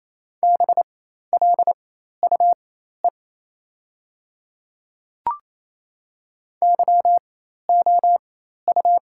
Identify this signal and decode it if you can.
Morse code, heard as a keyed tone, about 28 wpm, spaced as for 13 wpm, 700 Hz. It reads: BLUE YOU
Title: 4x Character Spacing - Top 500 Words - 28wpm